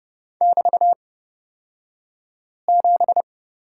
Morse code, heard as a keyed tone, about 30 wpm, spaced as for 5 wpm, 700 Hz. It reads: <BT>7
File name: Lesson 36d - BT - Two Characters 30wpm